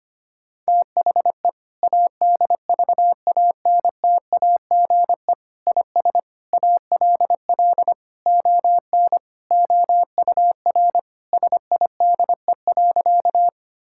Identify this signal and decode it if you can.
Morse code, heard as a keyed tone, about 25 wpm, 700 Hz. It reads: THE ADVANTAGE IS ALL ON OUR SIDE.
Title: Sentences from Top 1000 Words - Review Part 6 - 25wpm